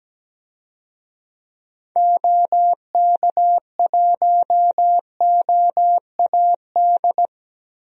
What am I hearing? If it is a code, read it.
Morse code, heard as a keyed tone, about 17 wpm, 700 Hz. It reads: OK1OAD
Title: DXCC-100 Call Signs - 6 Characters - 17wpm